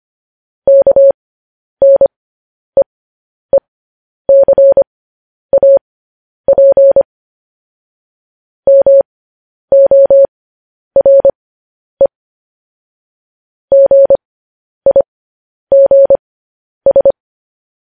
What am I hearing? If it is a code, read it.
Morse code, heard as a keyed tone, about 25 wpm, spaced as for 10 wpm, 550 Hz. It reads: KNEECAP MORE GIGS
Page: News Headlines in Morse Code at 10 WPM